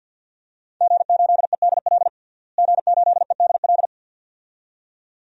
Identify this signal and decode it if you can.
Morse code, heard as a keyed tone, about 52 wpm, 700 Hz. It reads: G8EDL G8EDL
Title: DXCC-100 Call Signs - 5 Characters - Repeated 2x - 50wpm